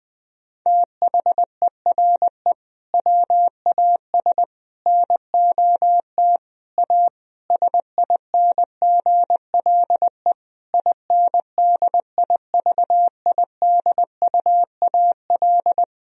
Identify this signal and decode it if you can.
Morse code, heard as a keyed tone, about 20 wpm, 700 Hz. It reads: THERE WAS NOT A SINGLE INDIVIDUAL
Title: Sentences from Top 1000 Words - Review Part 4 - 20wpm